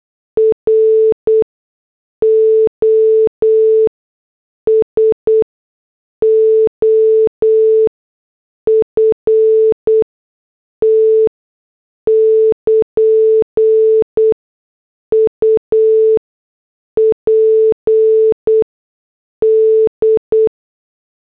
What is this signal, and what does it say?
Morse code, heard as a keyed tone, about 8 wpm, spaced as for 6 wpm, 440 Hz. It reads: ROSOFT <KN> UPD